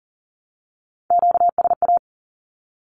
Morse code, heard as a keyed tone, about 40 wpm, 700 Hz. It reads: QSA
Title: Q-Codes 40wpm